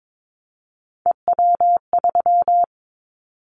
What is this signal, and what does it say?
Morse code, heard as a keyed tone, about 22 wpm, 700 Hz. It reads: EW3
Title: Callsign Prefixes for Top 100 DXCC - 22wpm